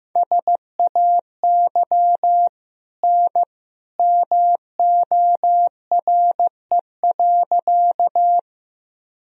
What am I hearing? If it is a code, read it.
Morse code, heard as a keyed tone, about 15 wpm, 700 Hz. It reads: SAY N MORE.
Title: Sentences from Top 200 Words and Abbreviations 15wpm